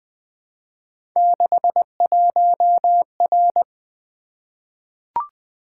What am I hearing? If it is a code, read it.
Morse code, heard as a keyed tone, about 20 wpm, 700 Hz. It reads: 61R
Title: Three Letter-Number Combinations - 20wpm